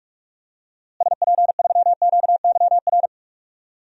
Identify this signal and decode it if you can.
Morse code, heard as a keyed tone, about 43 wpm, 700 Hz. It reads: SP3QYR
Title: DXCC-100 Call Signs - 6 Characters - Repeated 3x - 45wpm